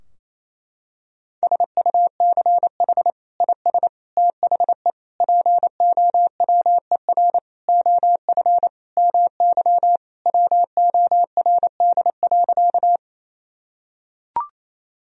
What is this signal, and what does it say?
Morse code, heard as a keyed tone, about 28 wpm, 700 Hz. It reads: SUCH IS THE POWER OF MY WORD.
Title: Sentences from Top 1000 Words - Review Part 2 - 28wpm